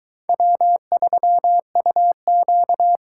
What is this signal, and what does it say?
Morse code, heard as a keyed tone, about 22 wpm, 700 Hz. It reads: W3UQ